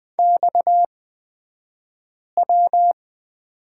Morse code, heard as a keyed tone, about 20 wpm, spaced as for 5 wpm, 700 Hz. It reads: XW
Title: 228 Lesson 37d - Letter x - Two Characters 20wpm